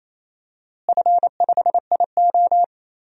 Morse code, heard as a keyed tone, about 28 wpm, 700 Hz. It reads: F5IO
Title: DXCC-100 Call Signs - 4 Characters - Repeated 3x - 28wpm